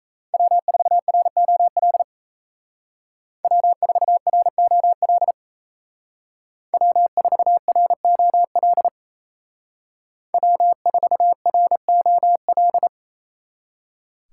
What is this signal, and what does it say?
Morse code, heard as a keyed tone, about 36 wpm, 700 Hz. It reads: W4ROL W4ROL W4ROL W4ROL